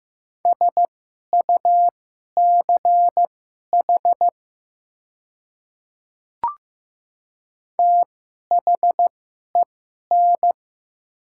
Morse code, heard as a keyed tone, about 15 wpm, spaced as for 11 wpm, 700 Hz. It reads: SUCH THEN
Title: 2x Character Spacing - Top 200 Words - 15wpm